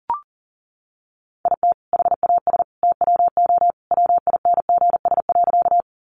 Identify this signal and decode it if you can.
Morse code, heard as a keyed tone, about 40 wpm, 700 Hz. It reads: IT HAS TWO WINGS.
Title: Additional Easy Phrases 40wpm